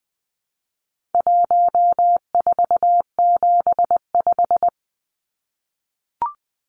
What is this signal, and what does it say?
Morse code, heard as a keyed tone, about 20 wpm, 700 Hz. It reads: 1475
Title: Numbers - 4-Digit 20wpm